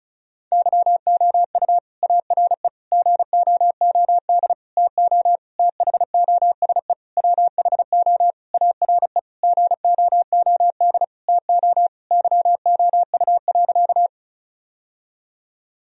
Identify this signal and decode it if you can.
Morse code, heard as a keyed tone, about 36 wpm, 700 Hz. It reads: YOU ARE GOOD TO THOSE WHO ARE GOOD TO YOU.
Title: Sentences from Top 1000 Words - Review Part 1 - 35wpm